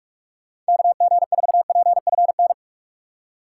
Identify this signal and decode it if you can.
Morse code, heard as a keyed tone, about 44 wpm, 700 Hz. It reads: KG4PFN
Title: USA Call Signs - 6 Characters - 2x Character Spacing - 45wpm